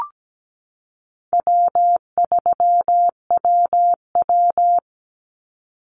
Morse code, heard as a keyed tone, about 17 wpm, 700 Hz. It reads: W3WW
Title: Easy Call Signs - 17wpm